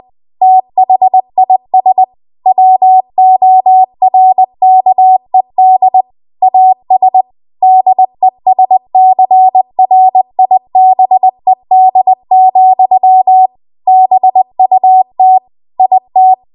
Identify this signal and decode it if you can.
Morse code, heard as a keyed tone, about 20 wpm, 750 Hz. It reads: THIS WORKED AS DESCRIBED, BUT IT
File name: Archive / 20 WPM / 20WPM